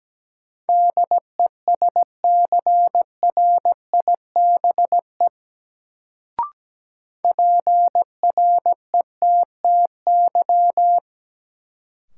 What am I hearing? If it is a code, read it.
Morse code, heard as a keyed tone, about 17 wpm, 700 Hz. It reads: DESCRIBE PRETTY
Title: Sentences from Top 1000 Words - New Word Drill - 17wpm